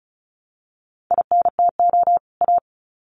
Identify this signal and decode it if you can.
Morse code, heard as a keyed tone, about 34 wpm, 700 Hz. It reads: INTO A
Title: Sets of 2 Words 35wpm